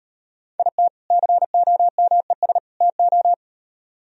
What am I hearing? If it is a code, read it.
Morse code, heard as a keyed tone, about 38 wpm, 700 Hz. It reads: IT COMES TO